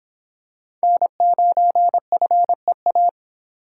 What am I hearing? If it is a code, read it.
Morse code, heard as a keyed tone, about 26 wpm, 700 Hz. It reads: N9FEA